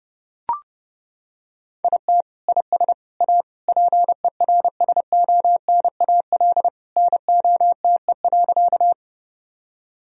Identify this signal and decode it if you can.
Morse code, heard as a keyed tone, about 30 wpm, 700 Hz. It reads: IT IS A PERSONAL NOTE.
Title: Sentences from Top 1000 Words - Review Part 4 - 30wpm